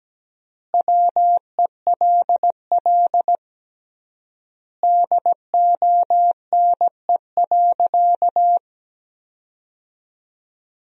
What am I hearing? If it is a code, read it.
Morse code, heard as a keyed tone, about 17 wpm, 700 Hz. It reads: WELL DONE.
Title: 3x Word-Spacing - Sentences from Top 500 Words - 17wpm